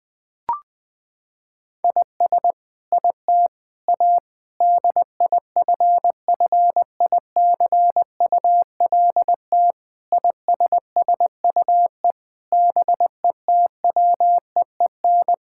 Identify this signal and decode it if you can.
Morse code, heard as a keyed tone, about 20 wpm, 700 Hz. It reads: IS IT A DIFFICULT ISSUE BETWEEN
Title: Sentences from Top 1000 Words - Review Part 6 - 20wpm